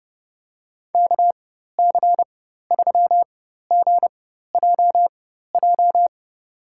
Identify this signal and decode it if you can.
Morse code, heard as a keyed tone, about 30 wpm, spaced as for 14 wpm, 700 Hz. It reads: KC3GJJ
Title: USA Call Signs - 4x Character Spacing - 30wpm